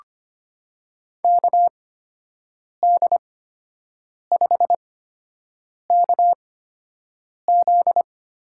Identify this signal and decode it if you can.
Morse code, heard as a keyed tone, about 25 wpm, spaced as for 7 wpm, 700 Hz. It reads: KD5KZ
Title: USA Call Signs - 5 Characters - 8x Character Spacing - 25wpm